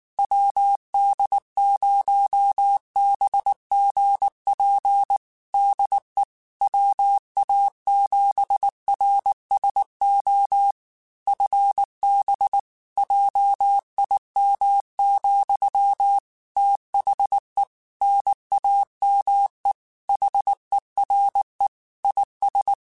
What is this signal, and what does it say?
Morse code, heard as a keyed tone, about 19 wpm, 800 Hz. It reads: WD0BGP DE WA7RSO FB JIM, THE NAME HERE IS
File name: QSOs / QSOs-mp3s / WD0BGP 1@18